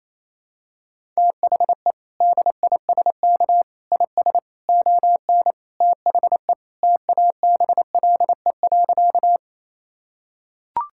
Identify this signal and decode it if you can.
Morse code, heard as a keyed tone, about 28 wpm, 700 Hz. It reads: THE DISK IS ON THE TABLE.
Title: Sentences from Top 1000 Words - Review Part 6 - 28wpm